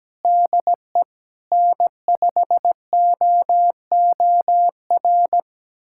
Morse code, heard as a keyed tone, about 17 wpm, 700 Hz. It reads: DE N5OOR
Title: USA Callsign DE Callsign - 5 Characters - 17wpm